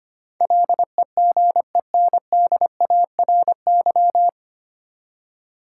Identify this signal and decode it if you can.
Morse code, heard as a keyed tone, about 25 wpm, 700 Hz. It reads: LEGENDARY
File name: Nine Letter Words - 25wpm